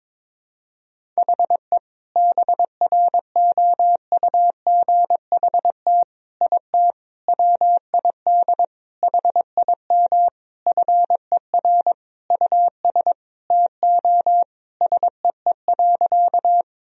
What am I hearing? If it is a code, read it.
Morse code, heard as a keyed tone, about 23 wpm, 700 Hz. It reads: HE BROUGHT IT WID HIM FER US TO SEE.
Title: Sentences from Top 500 Words and Abbreviations 22wpm